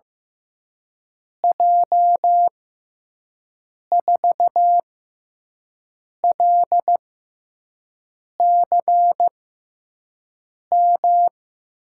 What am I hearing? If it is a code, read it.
Morse code, heard as a keyed tone, about 15 wpm, spaced as for 5 wpm, 700 Hz. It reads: J4LCM
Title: USA Call Signs - 6x Character Spacing - 15wpm